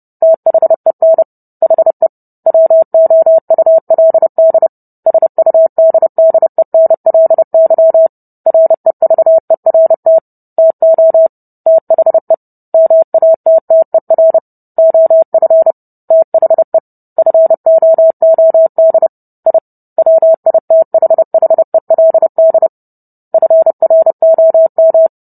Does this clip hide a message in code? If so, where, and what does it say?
Yes: Morse code from 0.2 to 25.1 s, heard as a keyed tone, about 30 wpm, 650 Hz. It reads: THEN HE WOULD SUDDENLY REVERT TO THE MATTER OF THE FOOD I WITHHELD FROM